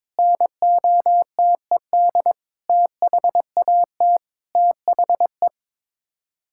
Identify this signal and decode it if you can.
Morse code, heard as a keyed tone, about 22 wpm, 700 Hz. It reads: NOTED THAT THE
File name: practice / 22wpm / Sets of 7 Words - 22wpm